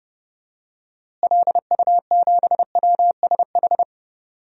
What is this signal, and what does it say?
Morse code, heard as a keyed tone, about 30 wpm, 700 Hz. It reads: LU7WSH
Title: DXCC-100 Call Signs - 6 Characters - Repeated 3x - 30wpm